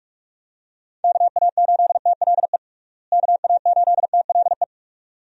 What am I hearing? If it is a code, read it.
Morse code, heard as a keyed tone, about 41 wpm, 700 Hz. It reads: KA8TLE KA8TLE
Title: USA Call Signs - 6 Characters - Repeated 2x - 45wpm